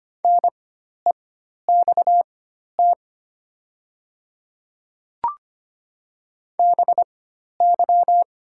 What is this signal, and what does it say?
Morse code, heard as a keyed tone, about 25 wpm, spaced as for 12 wpm, 690 Hz. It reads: NEXT BY
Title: 4x Character Spacing - Top 300 Words - 25wpm